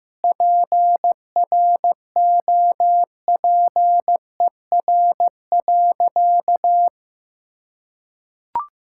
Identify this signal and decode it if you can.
Morse code, heard as a keyed tone, about 15 wpm, 700 Hz. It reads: PROPER.
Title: Sentences from Top 1000 Words - Review Part 5 - 15wpm